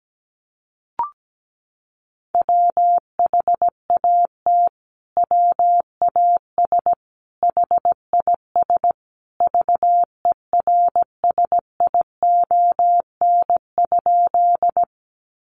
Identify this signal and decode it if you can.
Morse code, heard as a keyed tone, about 17 wpm, 700 Hz. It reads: WHAT WAS HIS VERSION?
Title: Sentences from Top 1400 Words 17wpm